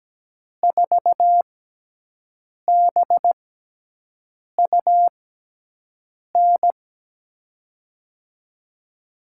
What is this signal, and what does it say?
Morse code, heard as a keyed tone, about 17 wpm, spaced as for 6 wpm, 700 Hz. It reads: 4BUN
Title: USA Call Signs - 5 Characters - 6x Character Spacing - 17wpm